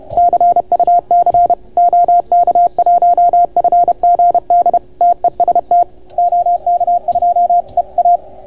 Morse code, heard as a keyed tone, about 30 wpm, 690 Hz, with radio noise. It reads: CUC OK1FGD TEST OK2EA